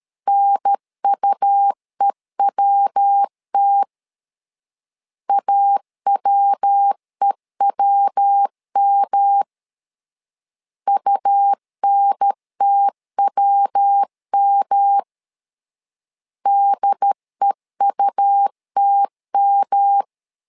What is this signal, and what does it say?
Morse code, heard as a keyed tone, about 13 wpm, 800 Hz. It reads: NUEWT AWEWM UNTWM DEUTM
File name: LEZIONI / 3 MWRO / 3 MW 12wpm 1633